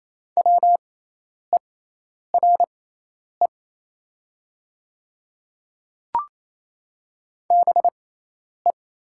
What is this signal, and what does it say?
Morse code, heard as a keyed tone, about 28 wpm, spaced as for 10 wpm, 700 Hz. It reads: WERE BE
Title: 6x Character Spacing - Top 200 Words - 28wpm